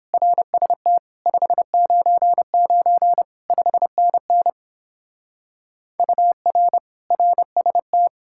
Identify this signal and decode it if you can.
Morse code, heard as a keyed tone, about 31 wpm, 700 Hz. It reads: RST 599 5NN UR RST